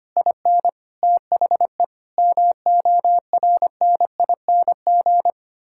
Morse code, heard as a keyed tone, about 25 wpm, 700 Hz. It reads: IN THE MORNING
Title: Pangram Challenge v2 - 25wpm